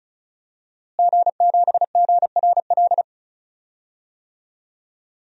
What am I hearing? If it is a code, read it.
Morse code, heard as a keyed tone, about 34 wpm, 700 Hz. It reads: G7GRL
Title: DXCC-100 Call Signs - 5 Characters - Repeated 2x - 35wpm